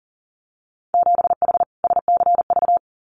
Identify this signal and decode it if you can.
Morse code, heard as a keyed tone, about 39 wpm, 700 Hz. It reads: 7H SCV
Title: Winter Field Day Exchanges Encore 40wpm